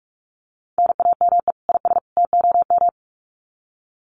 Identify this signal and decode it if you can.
Morse code, heard as a keyed tone, about 44 wpm, 700 Hz. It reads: NAME IS TOM
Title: Common QSO Phrases 45wpm